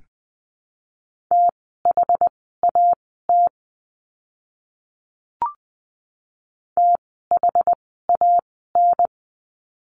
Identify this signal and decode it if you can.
Morse code, heard as a keyed tone, about 20 wpm, spaced as for 14 wpm, 700 Hz. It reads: THAT THAN
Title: 185 Lesson 30f - Letter b - Words 20wpm